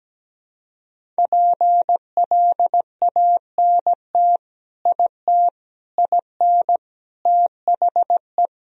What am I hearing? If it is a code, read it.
Morse code, heard as a keyed tone, about 17 wpm, 700 Hz. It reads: PLANT IT IN THE
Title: Sentences from Top 1000 Words - Review Part 4 - 17wpm